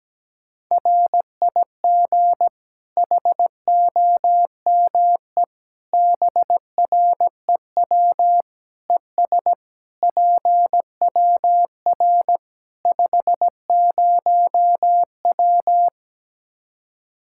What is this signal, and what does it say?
Morse code, heard as a keyed tone, about 17 wpm, 700 Hz. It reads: RIG HOME BREW ES PWR 50W
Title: Practice QSO - Exchange 2 - Info - 17wpm